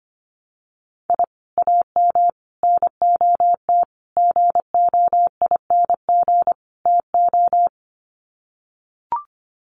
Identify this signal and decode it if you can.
Morse code, heard as a keyed tone, about 24 wpm, 700 Hz. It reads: I AM NOT GOING TO
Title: Sets of 5 Words Encore 25wpm